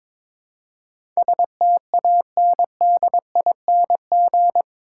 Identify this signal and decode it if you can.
Morse code, heard as a keyed tone, about 22 wpm, 700 Hz. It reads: STANDING
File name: practice / 22wpm / Eight Letter Words - 22wpm